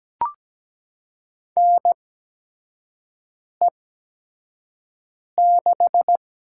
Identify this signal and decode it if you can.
Morse code, heard as a keyed tone, about 17 wpm, spaced as for 5 wpm, 700 Hz. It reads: NE6
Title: USA Call Signs - 4 Characters - 8x Character Spacing - 17wpm